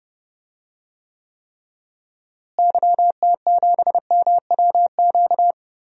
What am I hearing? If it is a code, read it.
Morse code, heard as a keyed tone, about 30 wpm, 700 Hz. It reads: YT7MWQ
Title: DXCC-100 Call Signs - 6 Characters - Repeated 3x - 30wpm